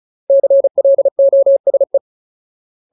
Morse code, heard as a keyed tone, about 34 wpm, 550 Hz. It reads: CLOSE
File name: Ultimate Word List / 35WPM Ultimate